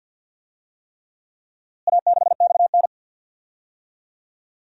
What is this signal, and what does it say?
Morse code, heard as a keyed tone, about 47 wpm, 700 Hz. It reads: A6XN
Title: DXCC-100 Call Signs - 4 Characters - Repeated 3x - 50wpm